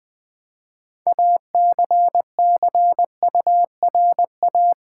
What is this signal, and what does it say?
Morse code, heard as a keyed tone, about 20 wpm, 700 Hz. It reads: ACCURA
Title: Build A Word - 2 Letters At A Time - 8-Letter Words - 20wpm